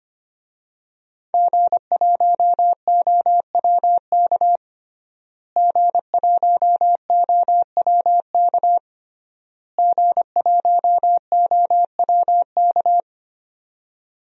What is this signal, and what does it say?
Morse code, heard as a keyed tone, about 25 wpm, 700 Hz. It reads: G1OWK G1OWK G1OWK
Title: DXCC-100 Call Signs - 5 Characters - Repeated 3x - 25wpm